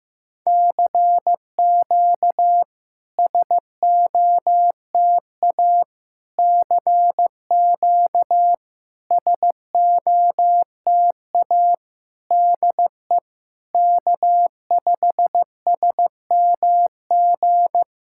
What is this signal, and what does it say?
Morse code, heard as a keyed tone, about 15 wpm, 700 Hz. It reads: CQ SOTA CQ SOTA DE K5SMG
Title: SOTA QSO Practice 15wpm